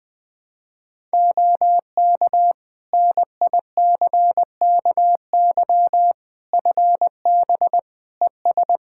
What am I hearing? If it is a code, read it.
Morse code, heard as a keyed tone, about 20 wpm, 700 Hz. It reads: OK NICKY FB ES
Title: Practice QSO - Exchange 2 - Info - 20wpm